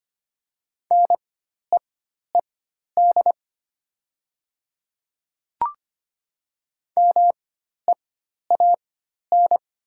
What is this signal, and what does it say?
Morse code, heard as a keyed tone, about 25 wpm, spaced as for 12 wpm, 700 Hz. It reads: NEED MEAN
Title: 4x Character Spacing - Top 200 Words - 25wpm